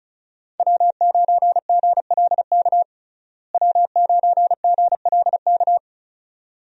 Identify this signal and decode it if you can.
Morse code, heard as a keyed tone, about 36 wpm, 700 Hz. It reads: W9GLK W9GLK
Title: USA Call Signs - 5 Characters - Repeated 2x - 35wpm